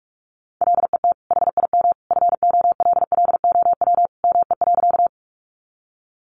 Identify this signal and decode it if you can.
Morse code, heard as a keyed tone, about 44 wpm, 700 Hz. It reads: LET HIM FOLLOW ME.
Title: Sentences from Top 300 Words 45wpm